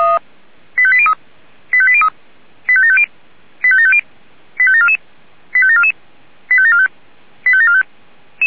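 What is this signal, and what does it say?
Five-tone selective call (ZVEI), heard as quick runs of five tones, tones of about 70 ms. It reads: 85803 85803 85750 85750 85744 85744 85745 85745
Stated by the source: ZVEI five-tone selective call 85730 received with an SDR.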